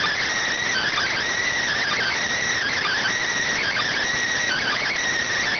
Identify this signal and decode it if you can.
Emergency Alert System (SAME) tones (header bursts), heard as harsh buzzing bursts. It is heard from 0.6 to 5.0 s.